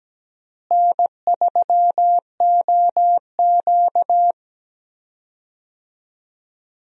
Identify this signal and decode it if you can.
Morse code, heard as a keyed tone, about 17 wpm, 700 Hz. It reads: N3OQ
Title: USA Call Signs - 4 Characters - Repeated 2x - 17wpm